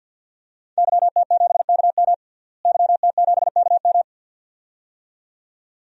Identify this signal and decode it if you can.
Morse code, heard as a keyed tone, about 50 wpm, 700 Hz. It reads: YT7KM YT7KM
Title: DXCC-100 Call Signs - 5 Characters - Repeated 2x - 50wpm